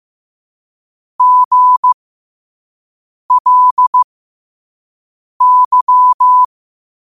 Morse code, heard as a keyed tone, about 15 wpm, spaced as for 5 wpm, 1000 Hz. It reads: GLY
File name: qotdfiles / 5wpm